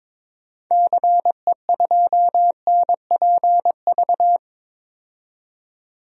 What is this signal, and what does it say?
Morse code, heard as a keyed tone, about 22 wpm, 700 Hz. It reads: CE2NPV
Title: DXCC-100 Call Signs - 6 Characters - 22wpm